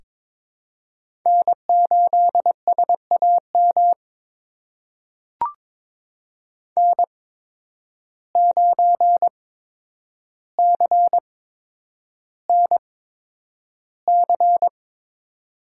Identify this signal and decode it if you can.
Morse code, heard as a keyed tone, about 22 wpm, spaced as for 9 wpm, 700 Hz. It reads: N8SAM N9CNC
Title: USA Call Signs - 5 Characters - 8x Character Spacing - 22wpm